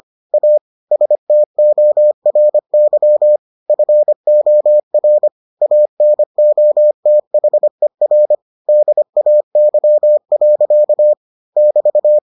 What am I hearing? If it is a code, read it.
Morse code, heard as a keyed tone, about 25 wpm, 600 Hz. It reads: A STORY FOR ANOTHER DAY. <BT>